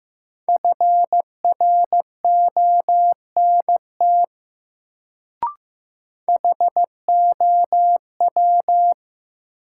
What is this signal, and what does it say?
Morse code, heard as a keyed tone, about 15 wpm, 700 Hz. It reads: FRONT HOW
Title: Sets of 2 Words - New Word Drill - 15wpm